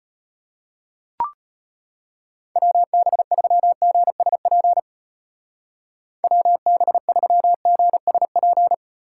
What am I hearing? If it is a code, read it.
Morse code, heard as a keyed tone, about 32 wpm, 700 Hz. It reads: WB3GSP WB3GSP